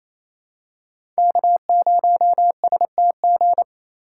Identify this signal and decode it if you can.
Morse code, heard as a keyed tone, about 28 wpm, 700 Hz. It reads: K0STG
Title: USA Call Signs - 5 Characters - 4x Character Spacing - 28wpm